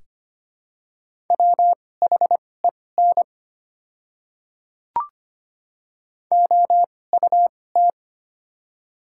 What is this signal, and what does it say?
Morse code, heard as a keyed tone, about 25 wpm, spaced as for 18 wpm, 700 Hz. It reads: WHEN OUT
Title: 2x Character Spacing - Top 100 Words - 25wpm